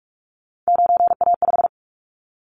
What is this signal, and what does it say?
Morse code, heard as a keyed tone, about 44 wpm, 700 Hz. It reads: 9A5
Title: Callsign Prefixes for Top 100 DXCC - 45wpm